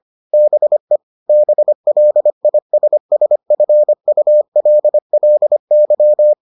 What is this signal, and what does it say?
Morse code, heard as a keyed tone, about 25 wpm, 600 Hz. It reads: BE BLISSFULLY